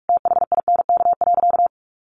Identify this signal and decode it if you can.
Morse code, heard as a keyed tone, about 44 wpm, 700 Hz. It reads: THINK.
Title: Sentences from Top 1100 Words 45wpm